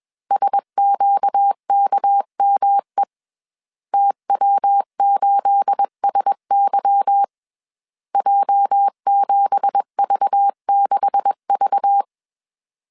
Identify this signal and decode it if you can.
Morse code, heard as a keyed tone, about 21 wpm, 800 Hz. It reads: SQKME TW8SY J7V6V